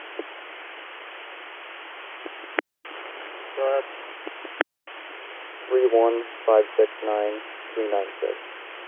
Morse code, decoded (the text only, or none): none